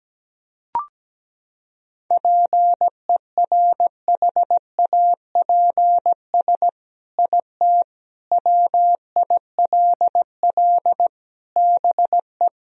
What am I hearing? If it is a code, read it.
Morse code, heard as a keyed tone, about 17 wpm, 700 Hz. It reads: PERHAPS IT WILL BE
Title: Sentences from Top 1000 Words - Review Part 3 - 17wpm